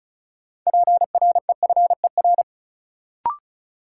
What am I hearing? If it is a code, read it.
Morse code, heard as a keyed tone, about 34 wpm, 700 Hz. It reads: PREFER
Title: Sentences from Top 1100 Words - New Word Drill - 35wpm